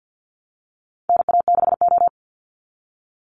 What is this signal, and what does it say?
Morse code, heard as a keyed tone, about 50 wpm, 700 Hz. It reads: NA6O
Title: NA SOTA Chaser Callsigns 50wpm